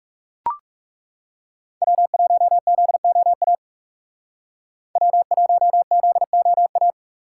Morse code, heard as a keyed tone, about 39 wpm, 700 Hz. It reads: W1ZOA W1ZOA